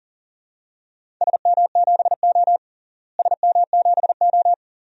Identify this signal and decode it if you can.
Morse code, heard as a keyed tone, about 40 wpm, 700 Hz. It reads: SM7O SM7O
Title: DXCC-100 Call Signs - 4 Characters - Repeated 2x - 40wpm